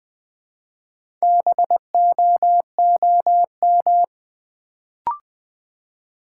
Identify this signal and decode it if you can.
Morse code, heard as a keyed tone, about 20 wpm, 700 Hz. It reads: BOOM
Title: Build A Word - 2 Letters At A Time - 4-Letter Words - 20wpm